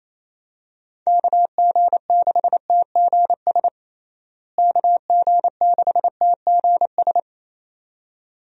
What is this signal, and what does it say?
Morse code, heard as a keyed tone, about 28 wpm, 700 Hz. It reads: KG6TGS KG6TGS
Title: USA Call Signs - 6 Characters - Repeated 2x - 28wpm